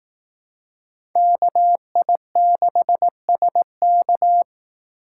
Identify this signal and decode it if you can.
Morse code, heard as a keyed tone, about 18 wpm, 700 Hz. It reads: KI6SK
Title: Speed Racing - USA 2x2 Call Signs 15wpm